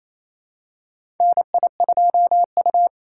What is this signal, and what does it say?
Morse code, heard as a keyed tone, about 28 wpm, 700 Hz. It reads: NI2U
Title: USA Call Signs - 4 Characters - 4x Character Spacing - 28wpm